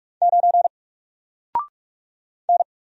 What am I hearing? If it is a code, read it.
Morse code, heard as a keyed tone, about 45 wpm, 700 Hz. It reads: 9 N
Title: Single Letter-Number Alt 45wpm